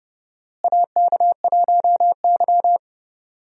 Morse code, heard as a keyed tone, about 30 wpm, 700 Hz. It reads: AK1Y
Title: USA Call Signs - 4 Characters - Repeated 3x - 30wpm